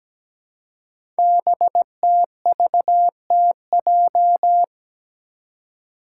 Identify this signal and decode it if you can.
Morse code, heard as a keyed tone, about 17 wpm, 700 Hz. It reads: BTVTJ
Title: Code Groups - 5 Letters - 17wpm